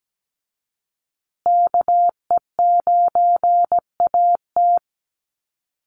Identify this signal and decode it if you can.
Morse code, heard as a keyed tone, about 17 wpm, 700 Hz. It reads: KE9AT
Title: USA Call Signs Repeated 2x 17wpm